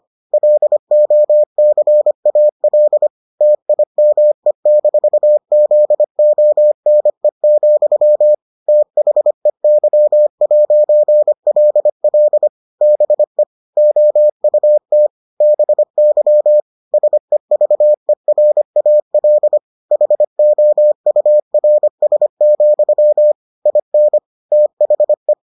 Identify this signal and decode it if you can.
Morse code, heard as a keyed tone, about 25 wpm, 600 Hz. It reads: LOCAL TIME-ZONE, THEY'LL BE OUT BY SEVERAL HOURS, IN THE